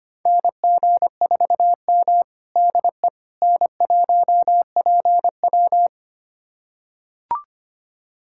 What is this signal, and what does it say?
Morse code, heard as a keyed tone, about 25 wpm, 700 Hz. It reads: NG4M DE N1PW
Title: USA Callsign DE Callsign - 4 Characters - 25wpm